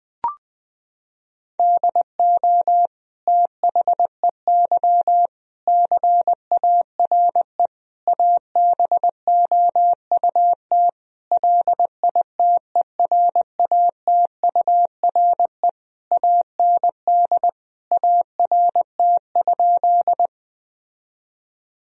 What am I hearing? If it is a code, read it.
Morse code, heard as a keyed tone, about 20 wpm, 700 Hz. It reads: DO THEY CARE ABOUT LITERATURE AND ART?
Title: Sentences from Top 1000 Words - Review Part 3 - 20wpm